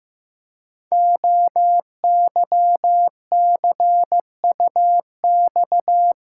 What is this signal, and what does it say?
Morse code, heard as a keyed tone, about 15 wpm, 700 Hz. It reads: OYCUX
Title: Code Groups - 5 Letter:Numbers - 15wpm